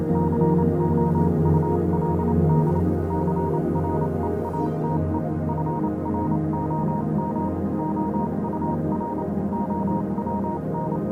Morse code, heard as a keyed tone, about 28 wpm, spaced as for 25 wpm, 940 Hz, with noise. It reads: COULD CREATE FROM OUR OWN
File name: Vylet Pony / aura / do you remember the song she sang (morse code version)